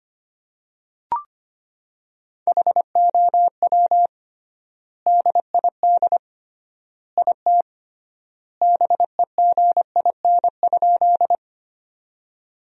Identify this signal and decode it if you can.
Morse code, heard as a keyed tone, about 25 wpm, 700 Hz. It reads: HOW DID IT BEGIN?
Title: 3x Word-Spacing - Sentences from Top 300 Words - 25wpm